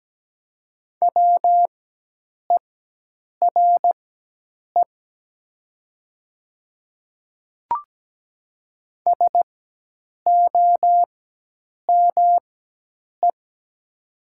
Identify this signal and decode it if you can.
Morse code, heard as a keyed tone, about 17 wpm, spaced as for 8 wpm, 700 Hz. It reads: WERE SOME
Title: 4x Character Spacing - Top 100 Words - 17wpm